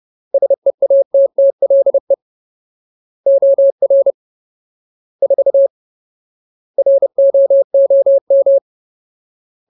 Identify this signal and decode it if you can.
Morse code, heard as a keyed tone, about 30 wpm, 550 Hz. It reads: SEATTLE OR 4 ROOM
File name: Ultimate Word List / 30WPM Ultimate